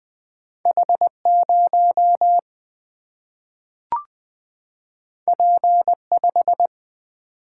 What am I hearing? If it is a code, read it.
Morse code, heard as a keyed tone, about 20 wpm, 700 Hz. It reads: H0 P5
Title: Single-Letter Single-Number 20wpm